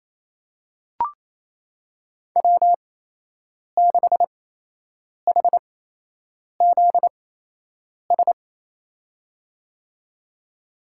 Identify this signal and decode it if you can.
Morse code, heard as a keyed tone, about 28 wpm, spaced as for 8 wpm, 700 Hz. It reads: W6HZS
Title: USA Call Signs - 5 Characters - 8x Character Spacing - 28wpm